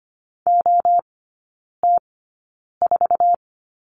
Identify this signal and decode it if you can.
Morse code, heard as a keyed tone, about 25 wpm, spaced as for 9 wpm, 690 Hz. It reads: OT4
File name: Lesson 15e - Number 5 - Three Characters 25wpm